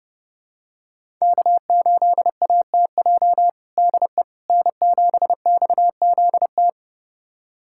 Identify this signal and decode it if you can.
Morse code, heard as a keyed tone, about 30 wpm, 700 Hz. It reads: K8ATJ DE N7XZT
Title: USA Callsign DE Callsign - 5 Characters - 30wpm